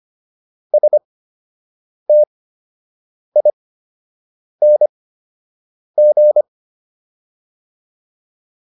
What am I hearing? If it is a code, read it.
Morse code, heard as a keyed tone, about 25 wpm, spaced as for 7 wpm, 600 Hz. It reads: STING